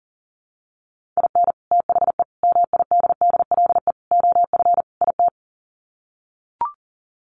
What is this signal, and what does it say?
Morse code, heard as a keyed tone, about 40 wpm, 700 Hz. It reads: IN THE MIDDLE OF IT
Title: Sets of 5 Words Encore 40wpm